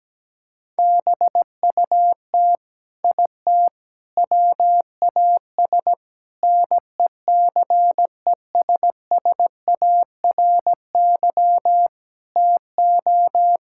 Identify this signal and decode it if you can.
Morse code, heard as a keyed tone, about 17 wpm, 700 Hz. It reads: BUT IT WAS NECESSARY TO
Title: Sentences from Top 1000 Words - Review Part 6 - 17wpm